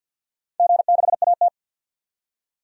Morse code, heard as a keyed tone, about 47 wpm, 700 Hz. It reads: G6AT
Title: DXCC-100 Call Signs - 4 Characters - 50wpm